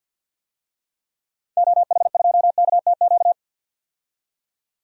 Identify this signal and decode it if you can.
Morse code, heard as a keyed tone, about 53 wpm, 700 Hz. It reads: OH2KTQ
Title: DXCC-100 Call Signs - 6 Characters - 50wpm